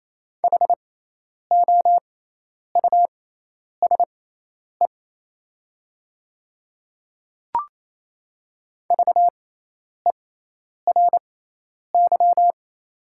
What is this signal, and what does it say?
Morse code, heard as a keyed tone, about 28 wpm, spaced as for 10 wpm, 700 Hz. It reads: HOUSE VERY
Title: 6x Character Spacing - Top 300 Words - 28wpm